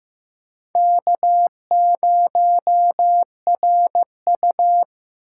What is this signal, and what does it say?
Morse code, heard as a keyed tone, about 15 wpm, 700 Hz. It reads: K0RU
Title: USA Call Signs - 4 Characters - 2x Character Spacing - 15wpm